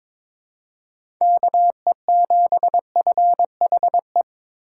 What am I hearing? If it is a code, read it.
Morse code, heard as a keyed tone, about 21 wpm, 700 Hz. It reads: KE7FHE
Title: USA Call Signs - 4x Character Spacing - 22wpm